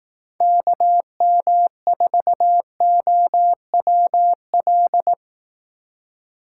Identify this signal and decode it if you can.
Morse code, heard as a keyed tone, about 18 wpm, 700 Hz. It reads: KM4OWL
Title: Speed Racing - USA Call Signs 15wpm